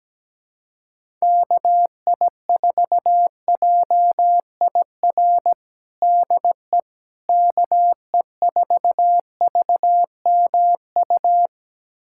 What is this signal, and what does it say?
Morse code, heard as a keyed tone, about 17 wpm, 700 Hz. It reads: KI4JIR DE KE4VMU
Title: USA Callsign DE Callsign - 6 Characters - 17wpm